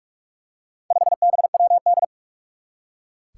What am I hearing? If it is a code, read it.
Morse code, heard as a keyed tone, about 46 wpm, 700 Hz. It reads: 5BWD
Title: Four Challenging Letter-Number Combinations 45wpm